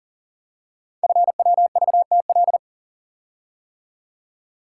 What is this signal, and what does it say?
Morse code, heard as a keyed tone, about 40 wpm, 700 Hz. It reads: FWVTL
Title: Code Groups - 5 Letter:Numbers - 40wpm